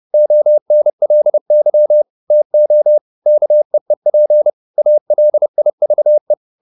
Morse code, heard as a keyed tone, about 30 wpm, 600 Hz. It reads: ONLY TO KEEP ALIVE